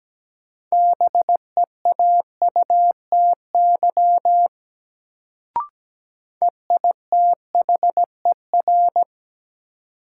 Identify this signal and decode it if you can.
Morse code, heard as a keyed tone, about 17 wpm, 700 Hz. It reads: BEAUTY EITHER
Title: Sentences from Top 600 Words - New Word Drill - 17wpm